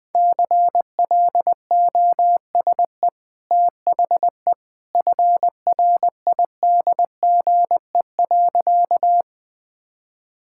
Morse code, heard as a keyed tone, about 21 wpm, 700 Hz. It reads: CLOSE THE FRIDGE.
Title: Easy Sentences with Hints - 20wpm